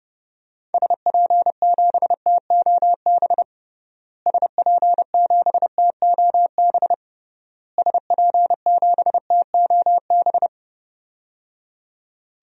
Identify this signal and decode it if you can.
Morse code, heard as a keyed tone, about 30 wpm, 700 Hz. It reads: SP7TOB SP7TOB SP7TOB
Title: DXCC-100 Call Signs - 6 Characters - Repeated 3x - 30wpm